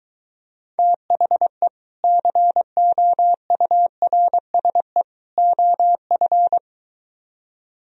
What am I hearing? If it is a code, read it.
Morse code, heard as a keyed tone, about 23 wpm, 700 Hz. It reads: THE COURSE OF